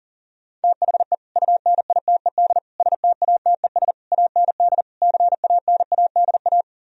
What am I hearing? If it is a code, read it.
Morse code, heard as a keyed tone, about 51 wpm, 700 Hz. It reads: THE UNITED STATES AND CANADA